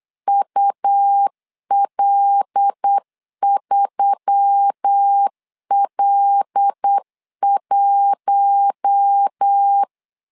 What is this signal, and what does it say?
Morse code, heard as a keyed tone, about 8 wpm, 800 Hz. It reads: UL3L1
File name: LEZIONI / 5 LBVC / 5 L 8wpm 2831